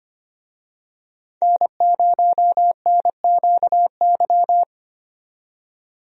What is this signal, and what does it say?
Morse code, heard as a keyed tone, about 25 wpm, 700 Hz. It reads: N0NQY
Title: USA Call Signs - 5 Characters - 25wpm